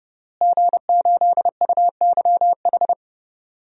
Morse code, heard as a keyed tone, about 30 wpm, 700 Hz. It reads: G8UYH
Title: DXCC-100 Call Signs - 5 Characters - Repeated 3x - 30wpm